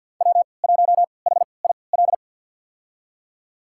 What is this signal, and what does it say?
Morse code, heard as a keyed tone, about 49 wpm, spaced as for 28 wpm, 700 Hz. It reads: W1HIL
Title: USA Call Signs - 3x Character Spacing - 50wpm